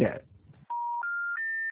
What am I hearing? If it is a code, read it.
Special information tone, with speech.
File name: out- -2006- - - .73713